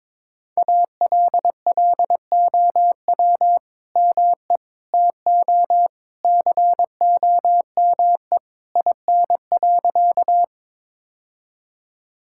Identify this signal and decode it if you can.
Morse code, heard as a keyed tone, about 22 wpm, 700 Hz. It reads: ALLOW ME TO COME IN.
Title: Sentences from Top 1000 Words - Review Part 5 - 22wpm